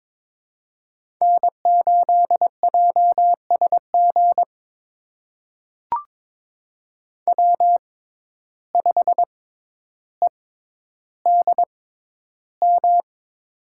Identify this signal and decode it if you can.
Morse code, heard as a keyed tone, about 22 wpm, spaced as for 11 wpm, 700 Hz. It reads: N8JSG W5EDM
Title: USA Call Signs - 5 Characters - 6x Character Spacing - 22wpm